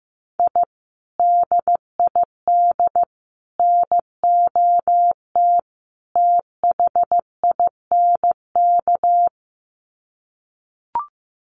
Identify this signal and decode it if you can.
Morse code, heard as a keyed tone, about 14 wpm, 700 Hz. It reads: I DID NOT THINK
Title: Sets of 4 Words 15wpm